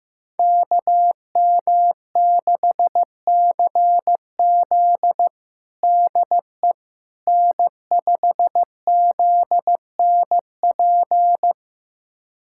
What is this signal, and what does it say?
Morse code, heard as a keyed tone, about 16 wpm, 700 Hz. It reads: KM6CZ DE N5ZNP
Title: USA Callsign DE Callsign - 5 Characters - 15wpm